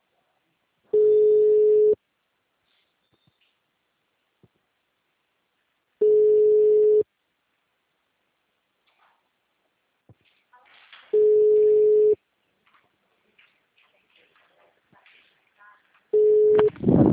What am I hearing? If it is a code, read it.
Ringback tone (Europe, 425 Hz).
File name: out- -2003- - - .21646